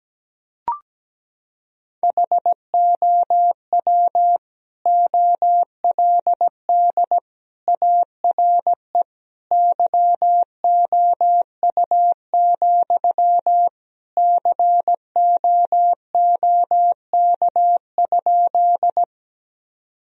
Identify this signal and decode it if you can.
Morse code, heard as a keyed tone, about 17 wpm, 700 Hz. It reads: HOW OLD ARE YOU, COOK?
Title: Sentences from Top 1000 Words - Review Part 4 - 17wpm